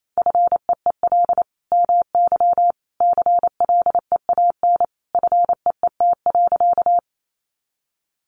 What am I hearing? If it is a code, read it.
Morse code, heard as a keyed tone, about 28 wpm, 700 Hz. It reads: FEEL MY CLEAN FEET.
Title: Sentences from Top 1100 Words 28wpm